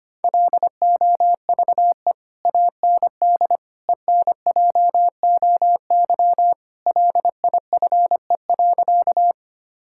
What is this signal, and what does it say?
Morse code, heard as a keyed tone, about 25 wpm, 700 Hz. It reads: LOVE AND ENJOY LIFE.
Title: Sentences from Top 1000 Words - Review Part 5 - 25wpm